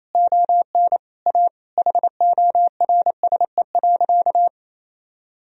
Morse code, heard as a keyed tone, about 28 wpm, 700 Hz. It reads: ON A HORSE.
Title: Sentences from Top 1000 Words - Review Part 1 - 28wpm